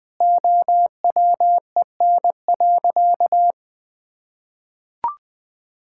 Morse code, heard as a keyed tone, about 21 wpm, 700 Hz. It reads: OWEN.
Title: Pangram Challenge v2 - 20wpm